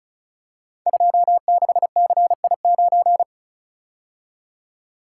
Morse code, heard as a keyed tone, about 36 wpm, 700 Hz. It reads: 26CI9
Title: Code Groups - 5 Letter:Numbers - 35wpm